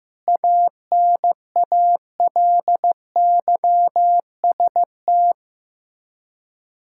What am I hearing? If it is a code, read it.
Morse code, heard as a keyed tone, about 15 wpm, 700 Hz. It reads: ANALYST
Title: Practice QSO - Exchange 3 - More Info - 15wpm